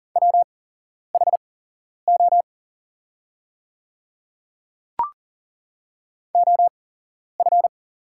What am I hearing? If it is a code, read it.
Morse code, heard as a keyed tone, about 41 wpm, spaced as for 11 wpm, 700 Hz. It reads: WHO OF
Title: 8x Character Spacing - Top 200 Words - 40wpm